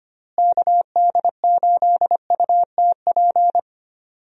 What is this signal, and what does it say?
Morse code, heard as a keyed tone, about 25 wpm, 700 Hz. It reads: KD8UTP
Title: USA Call Signs - 6 Characters - Repeated 2x - 25wpm